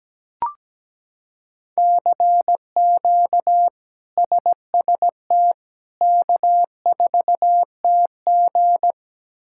Morse code, heard as a keyed tone, about 17 wpm, 700 Hz. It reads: CQ SST K4TG
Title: K1USN SST QSO Practice - 17wpm